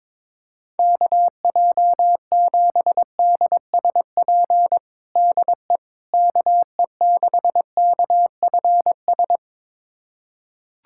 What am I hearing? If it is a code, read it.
Morse code, heard as a keyed tone, about 21 wpm, 700 Hz. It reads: KJ7DSP DE KE6KFS
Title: USA Callsign DE Callsign - 6 Characters - 22wpm